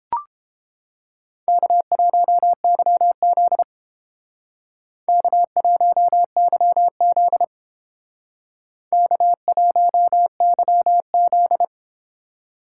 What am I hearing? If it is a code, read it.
Morse code, heard as a keyed tone, about 29 wpm, 700 Hz. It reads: K1YZ K1YZ K1YZ